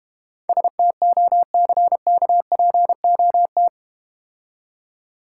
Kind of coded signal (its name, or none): Morse code